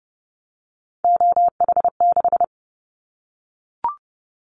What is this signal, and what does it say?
Morse code, heard as a keyed tone, about 30 wpm, 700 Hz. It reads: OH6
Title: Callsign Prefixes for Top 25 DXCC - Repeated 3x - 30wpm